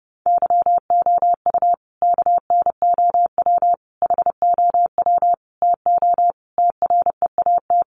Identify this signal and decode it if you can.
Morse code, heard as a keyed tone, about 30 wpm, 700 Hz. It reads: YOU KNOW HOW TO TREAT
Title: Sentences from Top 1300 Words 30wpm